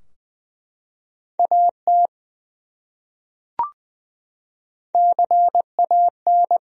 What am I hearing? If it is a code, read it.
Morse code, heard as a keyed tone, about 20 wpm, 700 Hz. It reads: AT CAN
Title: Top 100 Words Repeated 20wpm